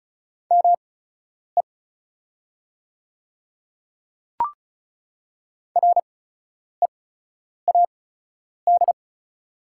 Morse code, heard as a keyed tone, about 36 wpm, spaced as for 10 wpm, 700 Hz. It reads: ME READ
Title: 8x Character Spacing - Top 300 Words - 35wpm